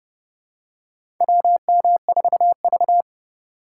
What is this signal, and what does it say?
Morse code, heard as a keyed tone, about 31 wpm, 700 Hz. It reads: WM4V
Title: USA Call Signs - 4 Characters - 4x Character Spacing - 30wpm